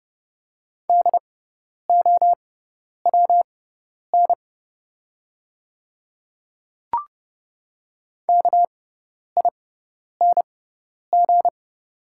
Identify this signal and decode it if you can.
Morse code, heard as a keyed tone, about 30 wpm, spaced as for 10 wpm, 700 Hz. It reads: DOWN KING
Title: 6x Character Spacing - Top 500 Words - 30wpm